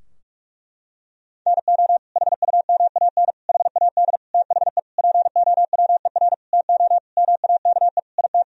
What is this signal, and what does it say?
Morse code, heard as a keyed tone, about 46 wpm, 700 Hz. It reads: NO HUMAN HAD THE POWER TO MAKE IT
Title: Sentences from Top 1000 Words - Review Part 3 - 45wpm